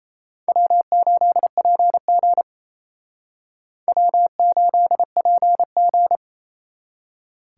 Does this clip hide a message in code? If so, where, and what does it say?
Yes: Morse code from 0.5 to 6.2 s, heard as a keyed tone, about 37 wpm, 700 Hz. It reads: W8PG W8PG